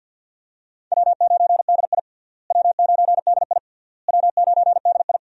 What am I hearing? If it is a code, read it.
Morse code, heard as a keyed tone, about 52 wpm, 700 Hz. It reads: W9DI W9DI W9DI
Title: USA Call Signs - 4 Characters - Repeated 3x - 50wpm